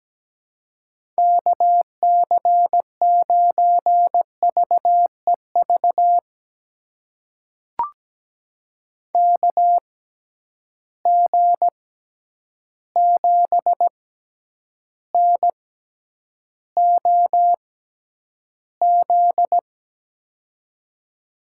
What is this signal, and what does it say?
Morse code, heard as a keyed tone, about 17 wpm, spaced as for 9 wpm, 700 Hz. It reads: KC9VEV KG7NOZ
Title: USA Call Signs - 6 Characters - 6x Character Spacing - 17wpm